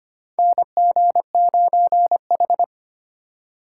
Morse code, heard as a keyed tone, about 25 wpm, 700 Hz. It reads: NG9H
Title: USA Call Signs - 4 Characters - 3x Character Spacing - 25wpm